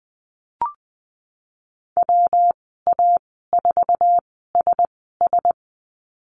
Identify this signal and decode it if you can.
Morse code, heard as a keyed tone, about 20 wpm, spaced as for 15 wpm, 700 Hz. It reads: WA4SS
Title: 147 Lesson 24g - Letter y - Call Signs 20wpm